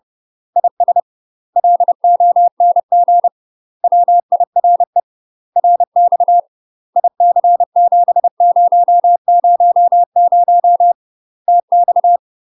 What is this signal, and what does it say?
Morse code, heard as a keyed tone, about 30 wpm, 700 Hz. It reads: IS LONG WIRE RX IC7000 TX
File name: UK MASTER QSOs TRAINING / 30WPM / UK MASTER QSOs 0027